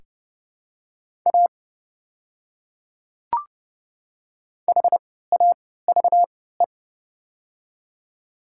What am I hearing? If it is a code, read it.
Morse code, heard as a keyed tone, about 30 wpm, spaced as for 17 wpm, 700 Hz. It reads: A HAVE
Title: 3x Character Spacing - Top 200 Words - 30wpm